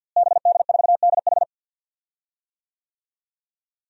Morse code, heard as a keyed tone, about 48 wpm, 700 Hz. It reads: BD4DH
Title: DXCC-100 Call Signs - 5 Characters - Repeated 2x - 50wpm